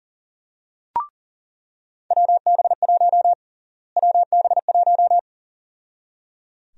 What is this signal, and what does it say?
Morse code, heard as a keyed tone, about 37 wpm, 700 Hz. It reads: WB1 WB1
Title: USA Call Sign Prefixes - Repeated 2x - 40wpm